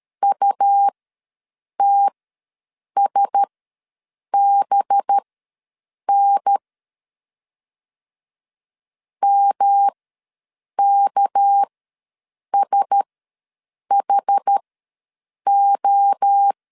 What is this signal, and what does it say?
Morse code, heard as a keyed tone, about 13 wpm, spaced as for 7 wpm, 800 Hz. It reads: UTSBN MKSHO
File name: LEZIONI / 6 PKX0 / 6 PKX0 ws 12wpm 3448